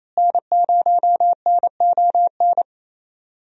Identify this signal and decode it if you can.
Morse code, heard as a keyed tone, about 28 wpm, 700 Hz. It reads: N0NON
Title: USA Call Signs - 5 Characters - Repeated 3x - 28wpm